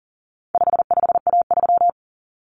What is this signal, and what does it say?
Morse code, heard as a keyed tone, about 40 wpm, 700 Hz. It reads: 55A3
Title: Common Cut Numbers - 4-Digit 40wpm